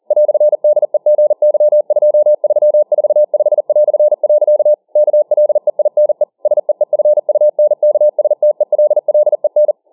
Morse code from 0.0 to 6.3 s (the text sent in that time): "DEGY2345". KLEINE